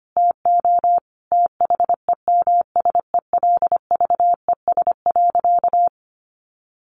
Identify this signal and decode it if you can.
Morse code, heard as a keyed tone, about 25 wpm, 700 Hz. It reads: TO THEMSELVES.
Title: Sentences from Top 1300 Words 25wpm